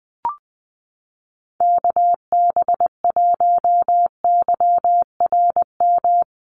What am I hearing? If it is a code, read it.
Morse code, heard as a keyed tone, about 20 wpm, 700 Hz. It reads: KB1YRM
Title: USA Call Signs Repeated 2x 20wpm